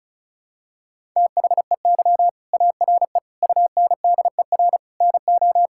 Morse code, heard as a keyed tone, about 34 wpm, 700 Hz. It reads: THEY ARE UNDER NO
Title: Sentences from Top 1000 Words - Review Part 6 - 35wpm